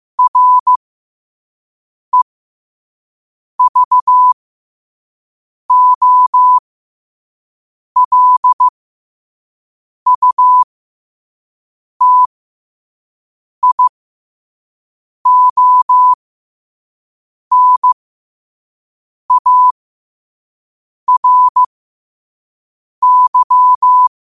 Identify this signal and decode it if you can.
Morse code, heard as a keyed tone, about 15 wpm, spaced as for 5 wpm, 1000 Hz. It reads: REVOLUTIONARY
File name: qotdfiles / 5wpm